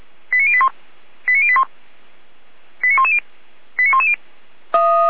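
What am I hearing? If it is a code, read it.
Five-tone selective call (ZVEI), heard as quick runs of five tones, tones of about 70 ms. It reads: 89081 89081 89110 89110
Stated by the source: ZVEI five-tone call received by SDR.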